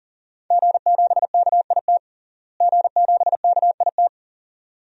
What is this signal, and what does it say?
Morse code, heard as a keyed tone, about 40 wpm, 700 Hz. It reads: G7KIT G7KIT
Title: DXCC-100 Call Signs - 5 Characters - Repeated 2x - 40wpm